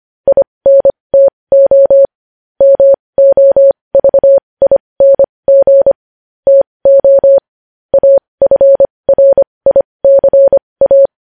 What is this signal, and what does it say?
Morse code, heard as a keyed tone, about 25 wpm, spaced as for 20 wpm, 550 Hz. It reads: INTO MOVING TO AFRICA
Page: News headlines in Morse code at 20 wpm